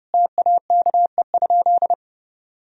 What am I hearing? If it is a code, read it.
Morse code, heard as a keyed tone, about 30 wpm, 700 Hz. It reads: TAKE?
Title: Sentences from Top 1000 Words - Review Part 1 - 30wpm